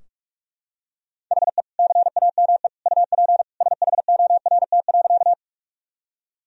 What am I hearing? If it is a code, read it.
Morse code, heard as a keyed tone, about 46 wpm, 700 Hz. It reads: HE CAME UP SHORT.
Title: Sentences from Top 1000 Words - Review Part 2 - 45wpm